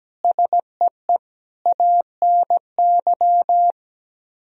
Morse code, heard as a keyed tone, about 17 wpm, 700 Hz. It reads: SEE ANY
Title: Speed-Racing - Sets of 5 Words 17wpm